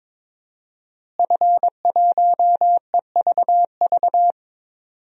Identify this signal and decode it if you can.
Morse code, heard as a keyed tone, about 22 wpm, 700 Hz. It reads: F1EVV
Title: DXCC-100 Call Signs - 5 Characters - Repeated 2x - 22wpm